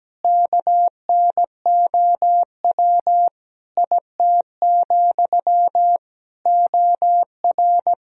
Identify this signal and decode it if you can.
Morse code, heard as a keyed tone, about 17 wpm, 700 Hz. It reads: KNOW IT, OR
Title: Sentences from Top 1000 Words - Review Part 4 - 17wpm